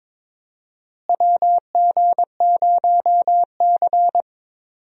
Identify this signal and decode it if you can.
Morse code, heard as a keyed tone, about 22 wpm, 700 Hz. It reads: WG0C
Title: USA Call Signs - 4 Characters - Repeated 2x - 22wpm